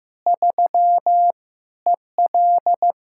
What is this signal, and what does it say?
Morse code, heard as a keyed tone, about 15 wpm, 700 Hz. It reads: 3 EL
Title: K1USN SST QSO Practice 15wpm